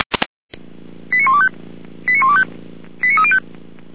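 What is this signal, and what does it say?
Five-tone selective call (ZVEI), heard as quick runs of five tones, tones of about 70 ms. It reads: 89125 89125 89305
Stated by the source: Five-tone ZVEI selective call received by SDR.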